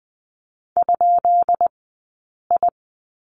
Morse code, heard as a keyed tone, about 20 wpm, spaced as for 8 wpm, 700 Hz. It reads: ?I
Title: 183 Lesson 30d - Letter b - Two Characters 20wpm